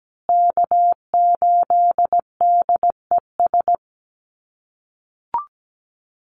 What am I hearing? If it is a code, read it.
Morse code, heard as a keyed tone, about 17 wpm, 700 Hz. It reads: K8DES
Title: USA Call Signs Repeated 3x 17wpm